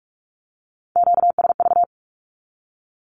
Morse code, heard as a keyed tone, about 44 wpm, 700 Hz. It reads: QSV
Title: Q-Codes 45wpm